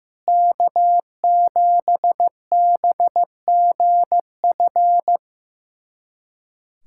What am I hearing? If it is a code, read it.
Morse code, heard as a keyed tone, about 15 wpm, 700 Hz. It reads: K7BGF
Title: USA Callsign DE Callsign - 5 Characters - 15wpm